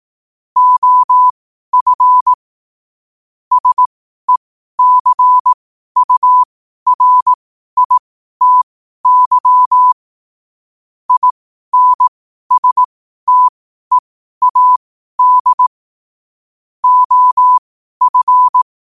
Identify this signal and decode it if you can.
Morse code, heard as a keyed tone, about 18 wpm, spaced as for 12 wpm, 1000 Hz. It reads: OF SECURITY INSTEAD OF
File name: qotdfiles / 12wpm